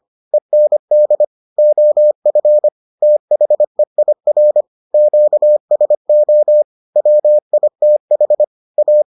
Morse code, heard as a keyed tone, about 25 wpm, 600 Hz. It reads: END OF THEIR QSO WITH A